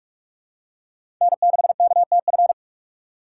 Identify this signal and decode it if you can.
Morse code, heard as a keyed tone, about 47 wpm, 700 Hz. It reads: N6KTF
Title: USA Call Signs - 5 Characters - 4x Character Spacing - 45wpm